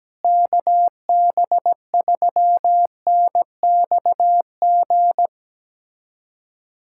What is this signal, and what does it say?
Morse code, heard as a keyed tone, about 17 wpm, 700 Hz. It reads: KB3NXG
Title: USA Callsign DE Callsign - 6 Characters - 17wpm